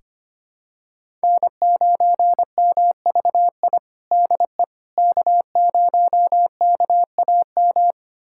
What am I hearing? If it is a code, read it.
Morse code, heard as a keyed tone, about 25 wpm, 700 Hz. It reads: N9MVI DE K0KAM
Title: USA Callsign DE Callsign - 5 Characters - 25wpm